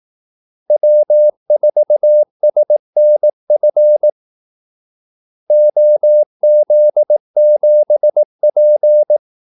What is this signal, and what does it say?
Morse code, heard as a keyed tone, about 18 wpm, 600 Hz. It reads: W4SNF OZ7P
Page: Kaldesignaler 18 wpm